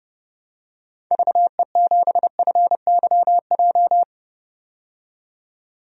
Morse code, heard as a keyed tone, about 30 wpm, 700 Hz. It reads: VE7FYJ
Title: DXCC-100 Call Signs - 6 Characters - 30wpm